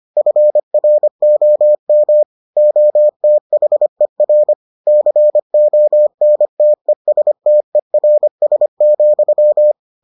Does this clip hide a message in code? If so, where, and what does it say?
Yes: Morse code from 0.2 to 9.7 s, heard as a keyed tone, about 25 wpm, 600 Hz. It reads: FROM OTHER CONTESTERS,